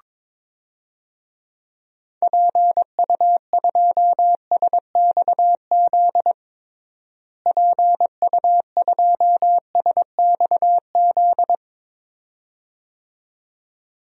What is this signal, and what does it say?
Morse code, heard as a keyed tone, about 22 wpm, 700 Hz. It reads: PU2SXZ PU2SXZ
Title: DXCC-100 Call Signs - 6 Characters - Repeated 2x - 22wpm